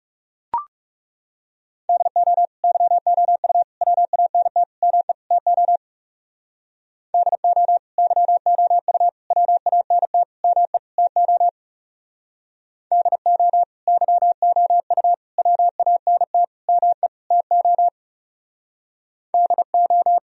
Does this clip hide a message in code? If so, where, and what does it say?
Yes: Morse code from 1.9 to 20.2 s, heard as a keyed tone, about 38 wpm, 700 Hz. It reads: DO YOU WANT ME TO DO YOU WANT ME TO DO YOU WANT ME TO DO